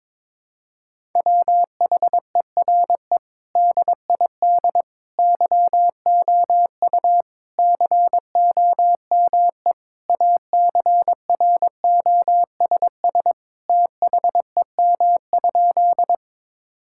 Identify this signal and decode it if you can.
Morse code, heard as a keyed tone, about 22 wpm, 700 Hz. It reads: WHERE DID YOU COME ACROSS THEM?
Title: Sentences from Top 1000 Words - Review Part 5 - 22wpm